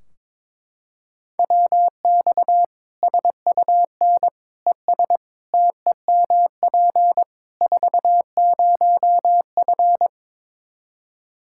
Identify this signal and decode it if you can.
Morse code, heard as a keyed tone, about 22 wpm, 700 Hz. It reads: WX SUN ES TEMP 40F
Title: Practice QSO - Exchange 2 - Info - 22wpm